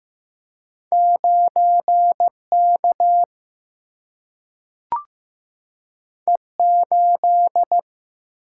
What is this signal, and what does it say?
Morse code, heard as a keyed tone, about 15 wpm, 700 Hz. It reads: 9K E8
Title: Single-Letter Single-Number 15wpm